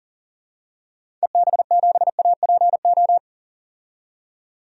Morse code, heard as a keyed tone, about 40 wpm, 700 Hz. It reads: EB7APO
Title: DXCC-100 Call Signs - 6 Characters - 40wpm